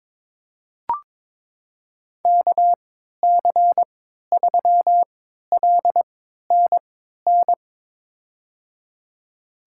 Morse code, heard as a keyed tone, about 22 wpm, spaced as for 13 wpm, 700 Hz. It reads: KC3LNN
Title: USA Call Signs - 6 Characters - 3x Character Spacing - 22wpm